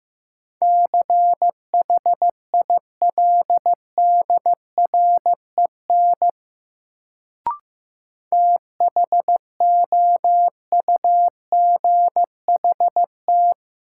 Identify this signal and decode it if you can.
Morse code, heard as a keyed tone, about 15 wpm, 700 Hz. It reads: CHILDREN THOUGHT
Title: Sentences from Top 300 Words - New Word Drill - 15wpm